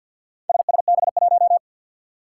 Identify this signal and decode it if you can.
Morse code, heard as a keyed tone, about 47 wpm, 700 Hz. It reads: SSB1
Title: Four Challenging Letter-Number Combinations 50wpm